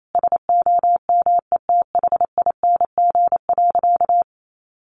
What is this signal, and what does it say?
Morse code, heard as a keyed tone, about 28 wpm, 700 Hz. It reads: SOMETHING.
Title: Sentences from Top 600 Words 28wpm